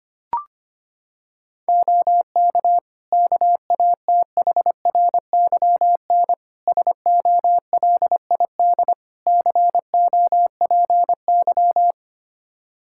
Morse code, heard as a keyed tone, about 25 wpm, 700 Hz. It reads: OK KATHRYN SOLID COPY
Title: K1USN SST QSO Practice 25wpm